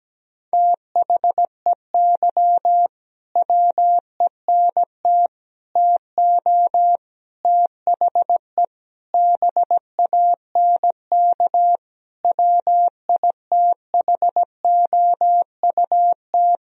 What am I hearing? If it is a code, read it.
Morse code, heard as a keyed tone, about 17 wpm, 700 Hz. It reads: THEY WENT TO THE BANK WITHOUT
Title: Sentences from Top 1000 Words - Review Part 4 - 17wpm